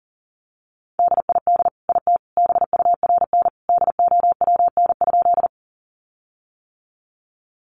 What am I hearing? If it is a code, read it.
Morse code, heard as a keyed tone, about 40 wpm, 700 Hz. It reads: DID IT BURN DOWN?
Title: Sentences from Top 1300 Words 40wpm